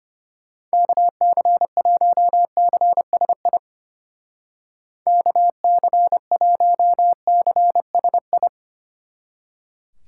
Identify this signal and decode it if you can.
Morse code, heard as a keyed tone, about 31 wpm, 700 Hz. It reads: KC1CSI KC1CSI
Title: Speed Racing - USA 2x3 Call Signs 25wpm